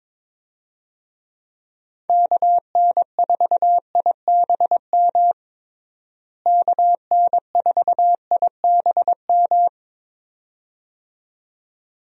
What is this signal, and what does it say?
Morse code, heard as a keyed tone, about 22 wpm, 700 Hz. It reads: KN4IBM KN4IBM
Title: USA Call Signs - 6 Characters - Repeated 2x - 22wpm